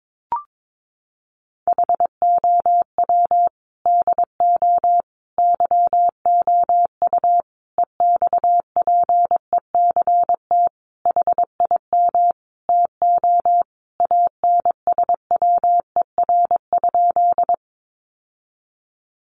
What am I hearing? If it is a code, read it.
Morse code, heard as a keyed tone, about 22 wpm, 700 Hz. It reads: HOW DO YOU EXPECT HIM TO ANSWER?
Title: Sentences from Top 1100 Words 22wpm